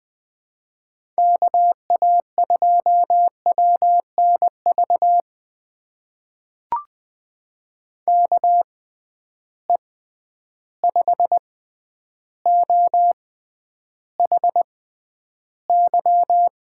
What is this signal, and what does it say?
Morse code, heard as a keyed tone, about 20 wpm, spaced as for 10 wpm, 700 Hz. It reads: KA2WNV KE5OHY
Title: USA Call Signs - 6 Characters - 6x Character Spacing - 20wpm